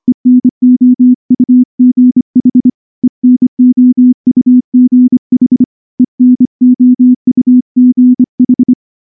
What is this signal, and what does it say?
Morse code, heard as a keyed tone, about 25 wpm, 250 Hz. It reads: ENOUGH ENOUGH ENOUGH
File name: morse / morse-training-25WPM